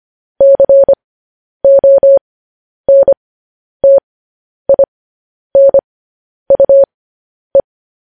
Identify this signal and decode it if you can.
Morse code, heard as a keyed tone, about 25 wpm, spaced as for 10 wpm, 550 Hz. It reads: CONTINUE